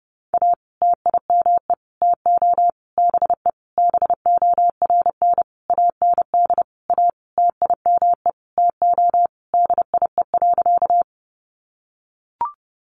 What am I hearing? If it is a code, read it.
Morse code, heard as a keyed tone, about 30 wpm, 700 Hz. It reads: A TIME TO BE BORN AND A TIME TO DIE.
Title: Sentences from Top 1300 Words 30wpm